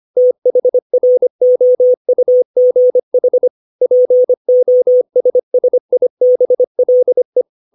Morse code, heard as a keyed tone, about 25 wpm, 500 Hz. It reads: THROUGH POSSIBLE